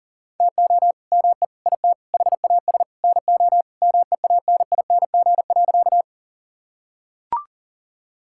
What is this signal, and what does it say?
Morse code, heard as a keyed tone, about 40 wpm, 700 Hz. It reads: TO ME IT HAS NO MEANING.
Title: Sentences from Top 1000 Words - Review Part 6 - 40wpm